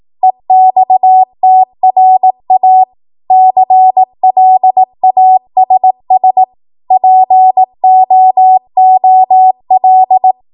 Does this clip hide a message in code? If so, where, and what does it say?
Yes: Morse code from 0.2 to 10.3 s, heard as a keyed tone, about 18 wpm, 750 Hz. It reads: EXTRA CLASS POOL